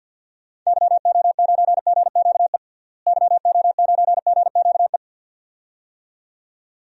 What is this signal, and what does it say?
Morse code, heard as a keyed tone, about 50 wpm, 700 Hz. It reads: YO9GQE YO9GQE
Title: DXCC-100 Call Signs - 6 Characters - Repeated 2x - 50wpm